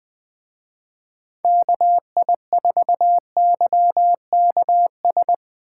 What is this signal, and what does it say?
Morse code, heard as a keyed tone, about 19 wpm, 700 Hz. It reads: KI4YKS
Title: Speed Racing - USA 2x3 Call Signs 17wpm